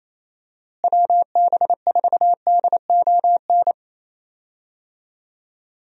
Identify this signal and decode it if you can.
Morse code, heard as a keyed tone, about 28 wpm, 700 Hz. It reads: WB4DON
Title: USA Call Signs - 6 Characters - Repeated 2x - 28wpm